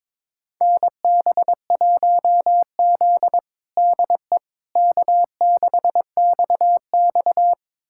Morse code, heard as a keyed tone, about 21 wpm, 700 Hz. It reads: NB1Z DE K6XX
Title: USA Callsign DE Callsign - 4 Characters - 22wpm